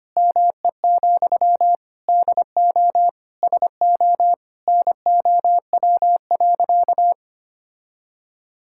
Morse code, heard as a keyed tone, about 25 wpm, 700 Hz. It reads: ME, DO SO NOW.
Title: Sentences from Top 1600 Words 25wpm